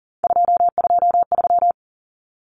Morse code, heard as a keyed tone, about 40 wpm, 700 Hz. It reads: 223
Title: Common Cut Numbers - 3-Digit 40wpm